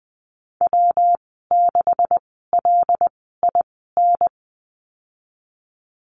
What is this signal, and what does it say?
Morse code, heard as a keyed tone, about 20 wpm, spaced as for 15 wpm, 700 Hz. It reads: W6LIN
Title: 147 Lesson 24g - Letter y - Call Signs 20wpm